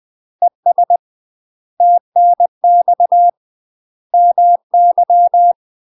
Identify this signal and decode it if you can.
Morse code, heard as a keyed tone, about 20 wpm, 700 Hz. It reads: ES TNX MY
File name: UK MASTER QSOs TRAINING / 20WPM / UK MASTER QSOs 0001